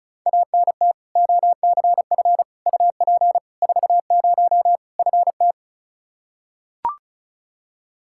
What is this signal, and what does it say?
Morse code, heard as a keyed tone, about 36 wpm, 700 Hz. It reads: ANT OCF UP 40 FT
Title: K1USN SST QSO Practice 35wpm